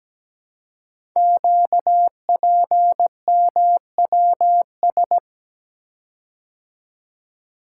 Morse code, heard as a keyed tone, about 17 wpm, 700 Hz. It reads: QPMWS
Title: Code Groups - 5 Letters - 17wpm